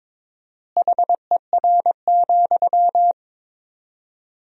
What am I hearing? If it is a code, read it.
Morse code, heard as a keyed tone, about 22 wpm, 700 Hz. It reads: HER,
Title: 4x Word-Spacing - Sentences from Top 200 Words - 22wpm